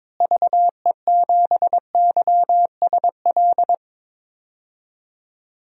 Morse code, heard as a keyed tone, about 22 wpm, 700 Hz. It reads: VE7YSL
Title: DXCC-100 Call Signs - 6 Characters - Repeated 2x - 22wpm